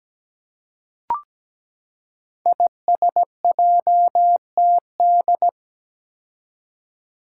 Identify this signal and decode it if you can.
Morse code, heard as a keyed tone, about 17 wpm, 700 Hz. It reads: ISJTD
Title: Five Challenging Letter-Number Combinations 17wpm